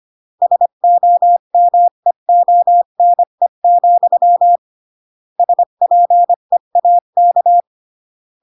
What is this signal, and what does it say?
Morse code, heard as a keyed tone, about 25 wpm, 700 Hz. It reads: SOMEONE, SPEAK